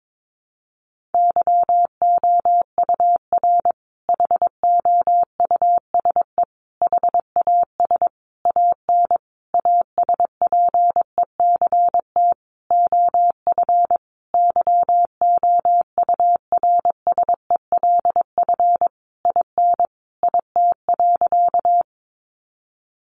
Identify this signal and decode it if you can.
Morse code, heard as a keyed tone, about 22 wpm, 700 Hz. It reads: YOUR HOUSE HAS AN ASPECT OF YOURSELF IN IT.
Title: Sentences from Top 1300 Words 22wpm